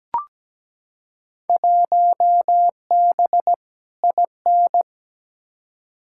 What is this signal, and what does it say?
Morse code, heard as a keyed tone, about 17 wpm, 700 Hz. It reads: 1B IN
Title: Field Day Exchanges - 17wpm